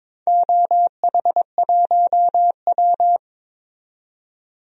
Morse code, heard as a keyed tone, about 21 wpm, 700 Hz. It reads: OH1W
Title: DXCC-100 Call Signs - 4 Characters - 22wpm